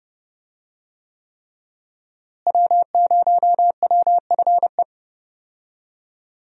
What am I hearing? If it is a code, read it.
Morse code, heard as a keyed tone, about 30 wpm, 700 Hz. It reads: W0WFE
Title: USA Call Signs - 5 Characters - Repeated 3x - 30wpm